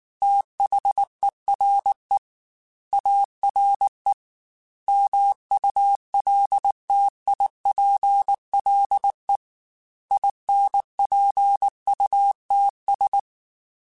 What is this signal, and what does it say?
Morse code, heard as a keyed tone, about 19 wpm, 800 Hz. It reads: THERE ARE MULTIPLE INPUTS
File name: Technical Subjects / Technical Subjects-mp3s / AND Gate @18